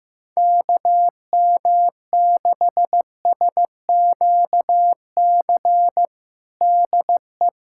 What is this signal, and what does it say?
Morse code, heard as a keyed tone, about 15 wpm, 700 Hz. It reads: KM6SQC DE
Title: USA Callsign DE Callsign - 6 Characters - 15wpm